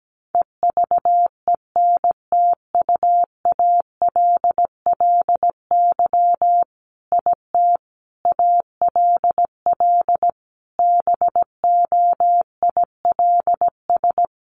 Morse code, heard as a keyed tone, about 17 wpm, 700 Hz. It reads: EVENTUALLY IT ALL BOILS
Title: A Sound Of Thunder 17wpm